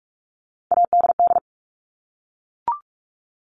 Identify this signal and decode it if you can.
Morse code, heard as a keyed tone, about 44 wpm, 700 Hz. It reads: ADD
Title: Top 200 Words Repeated 45wpm